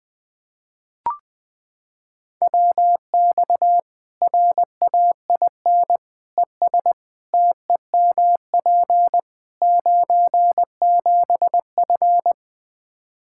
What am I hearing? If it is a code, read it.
Morse code, heard as a keyed tone, about 20 wpm, 700 Hz. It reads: WX RAIN ES TEMP 97F
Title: Practice QSO - Exchange 2 - Info - 20wpm